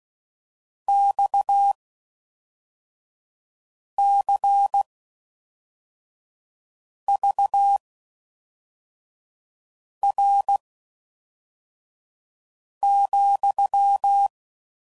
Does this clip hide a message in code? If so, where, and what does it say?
Yes: Morse code from 0.9 to 14.3 s, heard as a keyed tone, about 16 wpm, spaced as for 4 wpm, 800 Hz. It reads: XCVR,